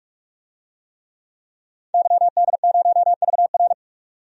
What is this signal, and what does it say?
Morse code, heard as a keyed tone, about 46 wpm, 700 Hz. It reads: YD0VR
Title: DXCC-100 Call Signs - 5 Characters - 45wpm